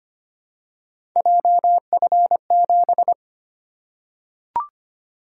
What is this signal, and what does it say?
Morse code, heard as a keyed tone, about 25 wpm, 700 Hz. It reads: JF7
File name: Three Letter-Number Combinations - 25wpm